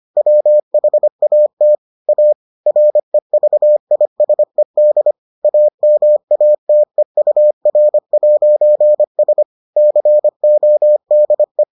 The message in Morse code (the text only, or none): WHAT A REVISED AMATEUR'S CODE